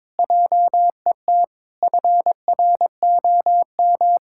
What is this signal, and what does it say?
Morse code, heard as a keyed tone, about 22 wpm, 700 Hz. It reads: JET FROM
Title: Pangram Challenge v2 - 22wpm